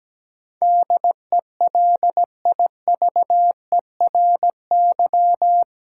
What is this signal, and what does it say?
Morse code, heard as a keyed tone, about 17 wpm, 700 Hz. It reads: DELIVERY
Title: Sentences from Top 1500 Words - New Word Drill - 17wpm